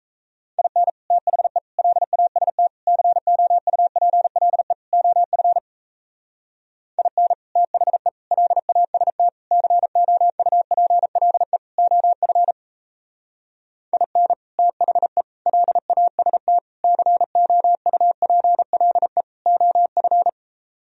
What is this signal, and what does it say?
Morse code, heard as a keyed tone, about 35 wpm, 700 Hz. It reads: IN THE LAST COUPLE OF IN THE LAST COUPLE OF IN THE LAST COUPLE OF